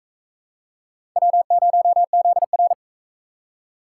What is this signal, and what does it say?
Morse code, heard as a keyed tone, about 40 wpm, 700 Hz. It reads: W0ZR